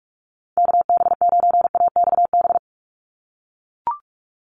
Morse code, heard as a keyed tone, about 44 wpm, 690 Hz. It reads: KB9AXB
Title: USA Call Signs 45wpm